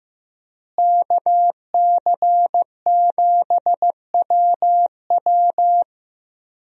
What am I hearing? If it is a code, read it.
Morse code, heard as a keyed tone, about 16 wpm, 700 Hz. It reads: KC7WW
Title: USA Call Signs - 5 Characters - Repeated 3x - 15wpm